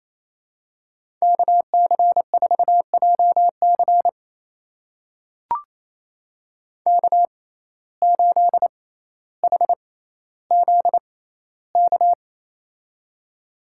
Morse code, heard as a keyed tone, about 28 wpm, spaced as for 14 wpm, 700 Hz. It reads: KC4JC K8HZK
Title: USA Call Signs - 6x Character Spacing - 28wpm